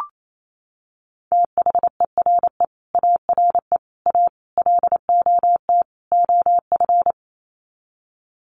Morse code, heard as a keyed tone, about 28 wpm, 700 Hz. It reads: THERE ARE A LOT OF
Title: Sets of 5 Words 28wpm